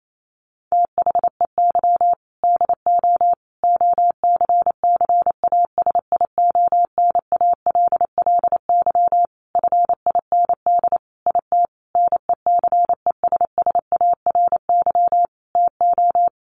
Morse code, heard as a keyed tone, about 28 wpm, 700 Hz. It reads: THEY DO OCCASIONALLY FIND IT NECESSARY TO
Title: Sentences from Top 1300 Words 28wpm